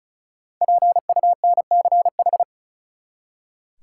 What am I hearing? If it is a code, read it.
Morse code, heard as a keyed tone, about 34 wpm, 700 Hz. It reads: PUNCH
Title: Five Letter Words - 35wpm